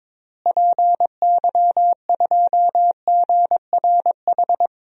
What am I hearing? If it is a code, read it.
Morse code, heard as a keyed tone, about 21 wpm, 700 Hz. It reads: PY2GRH
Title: DXCC-100 Call Signs - 6 Characters - Repeated 2x - 22wpm